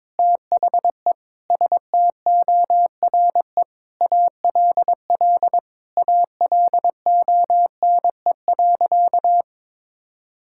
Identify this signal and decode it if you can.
Morse code, heard as a keyed tone, about 22 wpm, 700 Hz. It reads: THE STORE ALL ALONE.
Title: Sentences from Top 1000 Words - Review Part 5 - 22wpm